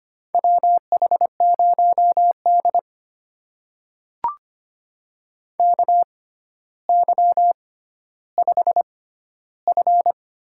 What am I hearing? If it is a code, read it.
Morse code, heard as a keyed tone, about 25 wpm, spaced as for 13 wpm, 700 Hz. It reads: WH0D KY5F
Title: USA Call Signs - 4 Characters - 6x Character Spacing - 25wpm